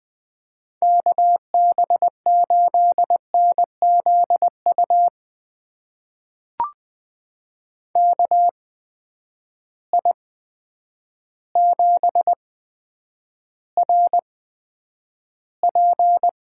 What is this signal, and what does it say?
Morse code, heard as a keyed tone, about 20 wpm, spaced as for 9 wpm, 700 Hz. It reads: KB8NZU KI7RP
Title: USA Call Signs - 6 Characters - 8x Character Spacing - 20wpm